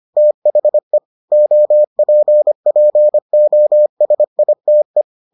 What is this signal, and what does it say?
Morse code, heard as a keyed tone, about 25 wpm, 600 Hz. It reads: THE OPPOSITE